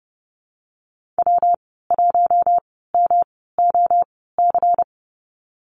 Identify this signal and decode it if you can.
Morse code, heard as a keyed tone, about 30 wpm, spaced as for 17 wpm, 700 Hz. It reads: W1MOC
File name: Lesson 19g - Letter w - Call Signs 30wpm